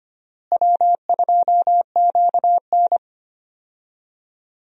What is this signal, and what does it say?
Morse code, heard as a keyed tone, about 25 wpm, 700 Hz. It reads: W2QN
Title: USA Call Signs - 4 Characters - Repeated 3x - 25wpm